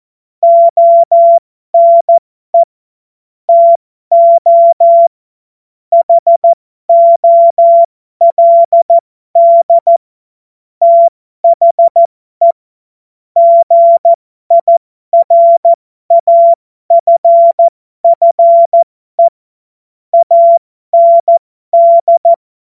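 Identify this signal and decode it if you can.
Morse code, heard as a keyed tone, about 14 wpm, spaced as for 12 wpm, 680 Hz. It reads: ONE TO HOLD THE GIRAFFE AND
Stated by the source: Morse code joke practice.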